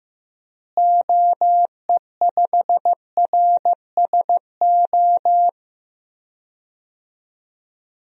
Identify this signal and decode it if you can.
Morse code, heard as a keyed tone, about 15 wpm, 700 Hz. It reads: OE5RSO
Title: DXCC-100 Call Signs - 6 Characters - Repeated 2x - 15wpm